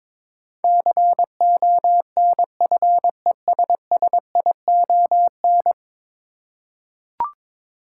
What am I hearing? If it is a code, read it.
Morse code, heard as a keyed tone, about 21 wpm, 700 Hz. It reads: CONFESSION
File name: practice / 22wpm / 10 Letter Words - First 3 Letters Given - 22wpm